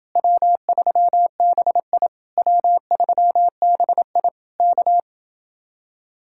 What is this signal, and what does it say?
Morse code, heard as a keyed tone, about 29 wpm, 700 Hz. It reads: W3BI W3BI K